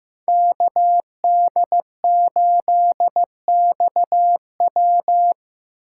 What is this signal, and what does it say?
Morse code, heard as a keyed tone, about 15 wpm, 700 Hz. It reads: KD8XW
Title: USA Call Signs - 5 Characters - 8x Character Spacing - 15wpm